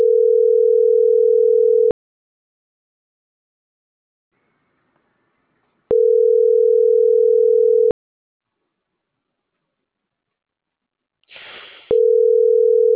Ringback tone (North America), 440 + 480 Hz.